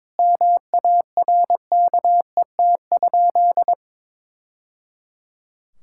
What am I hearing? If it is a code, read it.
Morse code, heard as a keyed tone, about 22 wpm, 700 Hz. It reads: MARKET?
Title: Sentences from Top 1000 Words - Review Part 5 - 22wpm